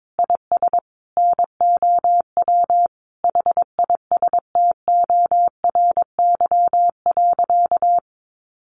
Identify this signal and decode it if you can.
Morse code, heard as a keyed tone, about 22 wpm, 700 Hz. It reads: IS NOW HISTORY.
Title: Sentences from Top 1500 Words 22wpm